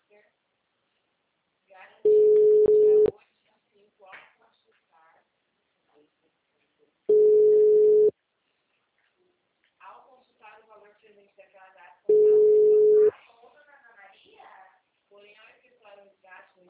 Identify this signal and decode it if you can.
Ringback tone (Europe, 425 Hz), with speech.